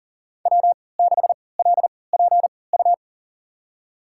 Morse code, heard as a keyed tone, about 40 wpm, spaced as for 23 wpm, 700 Hz. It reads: W6LPU
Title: USA Call Signs - 5 Characters - 3x Character Spacing - 40wpm